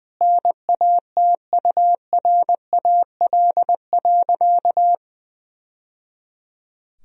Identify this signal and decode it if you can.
Morse code, heard as a keyed tone, about 19 wpm, 700 Hz. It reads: NATURAL.
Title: Sentences from Top 1000 Words - Review Part 4 - 20wpm